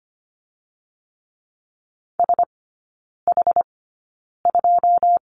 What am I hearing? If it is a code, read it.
Morse code, heard as a keyed tone, about 25 wpm, spaced as for 9 wpm, 700 Hz. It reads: SH2
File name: course / 25:10wpm / 093 Lesson 16e - Letter c - Three Characters 25wpm